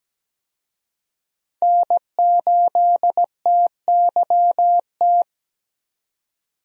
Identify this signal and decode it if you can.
Morse code, heard as a keyed tone, about 17 wpm, 700 Hz. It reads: N8TYT
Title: USA Call Signs - 5 Characters - 17wpm